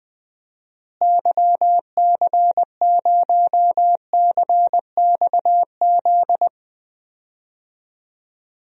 Morse code, heard as a keyed tone, about 20 wpm, 700 Hz. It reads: YC0CXZ
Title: DXCC-100 Call Signs - 6 Characters - Repeated 2x - 20wpm